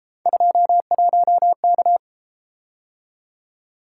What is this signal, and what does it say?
Morse code, heard as a keyed tone, about 30 wpm, 700 Hz. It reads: 21K